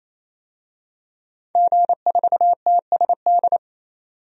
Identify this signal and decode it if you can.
Morse code, heard as a keyed tone, about 28 wpm, 700 Hz. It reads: G4TSD
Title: DXCC-100 Call Signs - 5 Characters - 28wpm